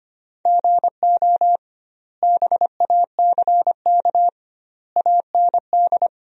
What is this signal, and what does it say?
Morse code, heard as a keyed tone, about 25 wpm, 700 Hz. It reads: GO BACK AND
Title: 2x Word-Spacing - Sentences from Top 300 Words - 25wpm